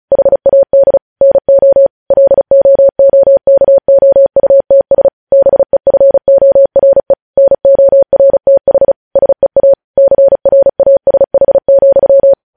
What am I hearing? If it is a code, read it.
Morse code, heard as a keyed tone, about 36 wpm, 550 Hz. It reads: HAD NO LOOKOUTS BEFORE NORTH SEA CRASH,
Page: news headlines in morse code at 35 wpm